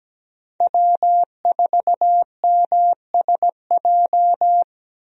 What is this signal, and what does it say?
Morse code, heard as a keyed tone, about 17 wpm, 700 Hz. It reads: W4MSJ
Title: USA Call Signs - 5 Characters - Repeated 2x - 17wpm